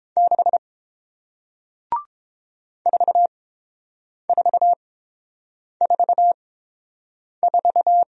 Morse code, heard as a keyed tone, about 30 wpm, spaced as for 7 wpm, 700 Hz. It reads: 6 4444